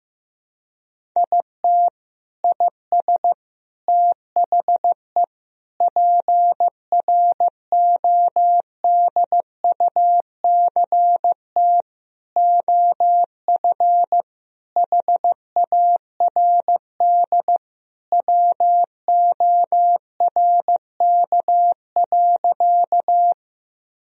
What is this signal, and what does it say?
Morse code, heard as a keyed tone, about 15 wpm, 700 Hz. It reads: IT IS THE PRODUCT OF HARD WORK.
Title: Sentences from Top 1000 Words - Review Part 2 - 15wpm